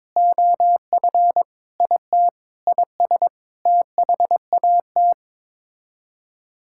Morse code, heard as a keyed tone, about 21 wpm, 700 Hz. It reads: OF IT IS THAT
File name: practice / 22wpm / Sets of 8 Words - 22wpm